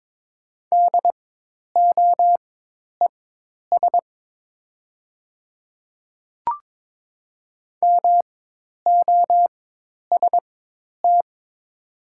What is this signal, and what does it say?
Morse code, heard as a keyed tone, about 22 wpm, spaced as for 10 wpm, 700 Hz. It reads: DOES MOST
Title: 4x Character Spacing - Top 300 Words - 22wpm